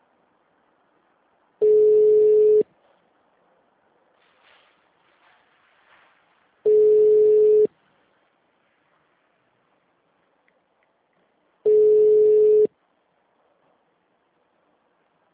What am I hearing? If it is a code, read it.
Ringback tone (Europe, 425 Hz).